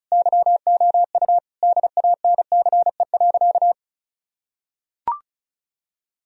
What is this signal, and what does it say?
Morse code, heard as a keyed tone, about 36 wpm, 700 Hz. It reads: YOU DANCE.
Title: Sentences from Top 1000 Words - Review Part 4 - 35wpm